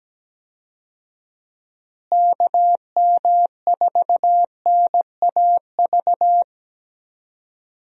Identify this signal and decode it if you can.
Morse code, heard as a keyed tone, about 17 wpm, 700 Hz. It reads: KM4NAV
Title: USA Call Signs - 6 Characters - Repeated 3x - 17wpm